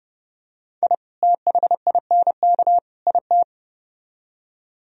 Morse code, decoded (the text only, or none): I THINK IT